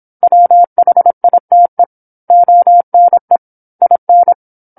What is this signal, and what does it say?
Morse code, heard as a keyed tone, about 27 wpm, 700 Hz. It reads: WHITE ONE IN
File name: 08-Chapter-08 26 WPM